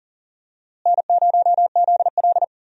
Morse code, heard as a keyed tone, about 40 wpm, 700 Hz. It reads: N0ZL
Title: USA Call Signs - 4 Characters - Repeated 2x - 40wpm